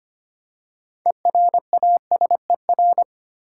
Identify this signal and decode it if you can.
Morse code, heard as a keyed tone, about 25 wpm, 690 Hz. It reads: ERASER
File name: practice / 25wpm / Easy Words with Hints - 25wpm